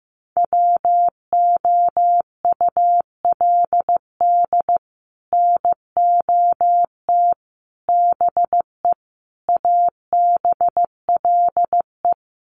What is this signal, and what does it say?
Morse code, heard as a keyed tone, about 15 wpm, 700 Hz. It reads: WOULD NOT BE ABLE
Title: Sets of 5 Words 15wpm